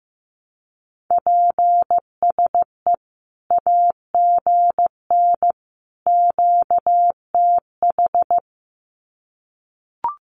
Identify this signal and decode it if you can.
Morse code, heard as a keyed tone, about 15 wpm, 700 Hz. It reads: PSE AGN QTH
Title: Common QSO Phrases 15wpm